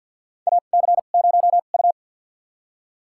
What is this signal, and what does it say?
Morse code, heard as a keyed tone, about 52 wpm, spaced as for 37 wpm, 690 Hz. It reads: AC0U
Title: USA Call Signs - 2x Character Spacing - 50wpm